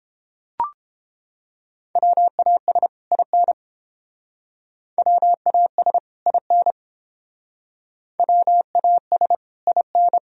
Morse code, heard as a keyed tone, about 30 wpm, 700 Hz. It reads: WAS IN WAS IN WAS IN